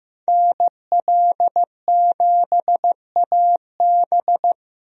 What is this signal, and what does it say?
Morse code, heard as a keyed tone, about 15 wpm, 700 Hz. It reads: NL7AB
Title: USA Call Signs - 5 Characters - 15wpm